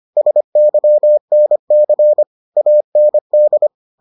Morse code, heard as a keyed tone, about 25 wpm, 600 Hz. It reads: SYNC AND